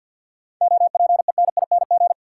Morse code, heard as a keyed tone, about 50 wpm, 700 Hz. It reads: OPENING
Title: Seven Letter Words - 50wpm